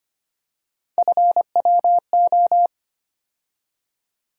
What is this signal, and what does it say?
Morse code, heard as a keyed tone, about 25 wpm, 700 Hz. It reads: FWO
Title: USA Call Sign Postfixes - 3 Letters 25wpm